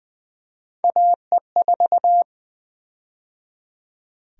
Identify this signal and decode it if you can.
Morse code, heard as a keyed tone, about 20 wpm, 700 Hz. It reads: AE4
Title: USA Call Sign Prefixes - 20wpm